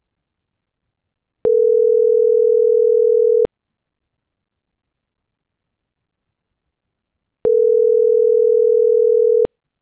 Ringback tone (North America), 440 + 480 Hz.